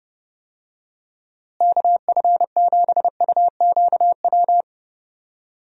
Morse code, heard as a keyed tone, about 30 wpm, 700 Hz. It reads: KF7UQW
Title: USA Call Signs - 6 Characters - Repeated 2x - 30wpm